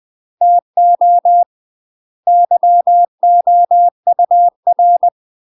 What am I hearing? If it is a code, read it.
Morse code, heard as a keyed tone, about 20 wpm, 700 Hz. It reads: TO YOUR
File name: 20WPM